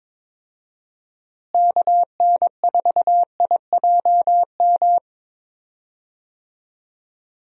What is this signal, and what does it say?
Morse code, heard as a keyed tone, about 22 wpm, 700 Hz. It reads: KN4IJM
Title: USA Call Signs - 6 Characters - Repeated 2x - 22wpm